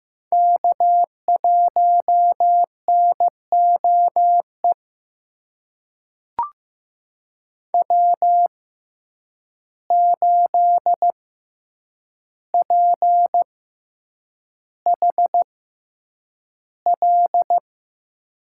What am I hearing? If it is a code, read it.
Morse code, heard as a keyed tone, about 15 wpm, spaced as for 8 wpm, 700 Hz. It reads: K1NOE W8PHL
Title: USA Call Signs - 5 Characters - 6x Character Spacing - 15wpm